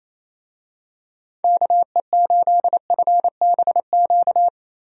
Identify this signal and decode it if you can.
Morse code, heard as a keyed tone, about 27 wpm, 700 Hz. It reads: KE8FBQ
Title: USA Call Signs - 6 Characters - 28wpm